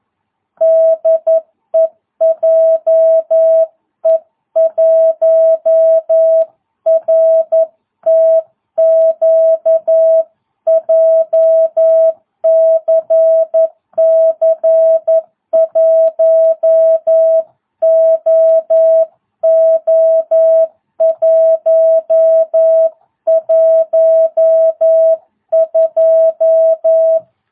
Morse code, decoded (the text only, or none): DE JE1RTQ JCC 1OO112